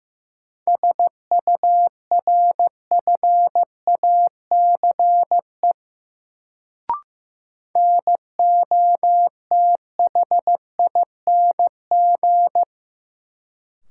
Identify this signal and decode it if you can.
Morse code, heard as a keyed tone, about 15 wpm, 700 Hz. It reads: SURFACE NOTHING
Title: Sets of 4 Words - New Word Drill - 15wpm